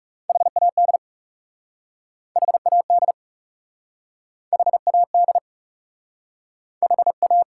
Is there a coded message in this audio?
Yes: Morse code from 0.3 to 7.4 s, heard as a keyed tone, about 40 wpm, 700 Hz. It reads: HAD HAD HAD H A